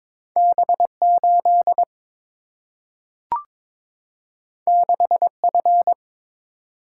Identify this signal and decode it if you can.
Morse code, heard as a keyed tone, about 22 wpm, 700 Hz. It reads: B 8 6 F
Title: Single-Letter Single-Number 22wpm